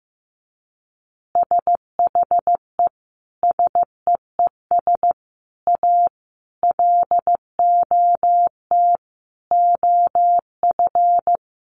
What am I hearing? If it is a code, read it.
Morse code, heard as a keyed tone, about 15 wpm, 700 Hz. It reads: SHE SEES A LOT OF
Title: Easy Phrases 15wpm